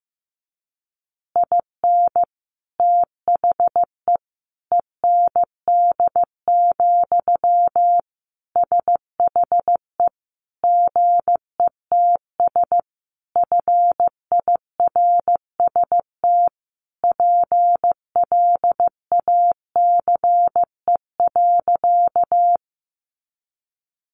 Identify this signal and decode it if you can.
Morse code, heard as a keyed tone, about 15 wpm, 700 Hz. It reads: IN THE END, SHE GETS FIRST PLACE.
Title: Additional Easy Phrases 15wpm